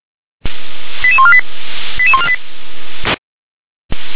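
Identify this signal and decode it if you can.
Five-tone selective call (ZVEI), heard as quick runs of five tones, tones of about 70 ms. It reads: 88148 88148
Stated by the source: ZVEI five-tone selective call 88080 received on an SDR.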